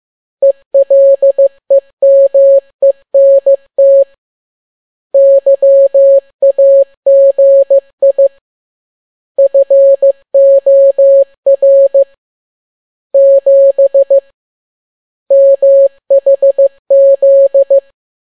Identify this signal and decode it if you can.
Morse code, heard as a keyed tone, about 15 wpm, 550 Hz. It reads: ELEMENT YAGI FOR 7 MHZ